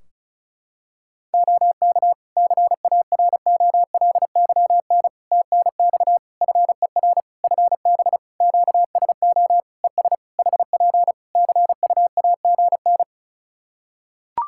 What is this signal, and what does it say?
Morse code, heard as a keyed tone, about 34 wpm, 700 Hz. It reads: OK CAROLYN TNX FER FB QSO ES HP CUAGN
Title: Practice QSO - Exchange 4 - Ending - 35wpm